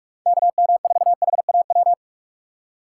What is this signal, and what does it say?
Morse code, heard as a keyed tone, about 46 wpm, 700 Hz. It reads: KM4HAW